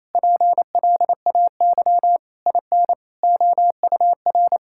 Morse code, heard as a keyed tone, about 28 wpm, 700 Hz. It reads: PLAY IN OUR
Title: Sentences from Top 1000 Words - Review Part 6 - 28wpm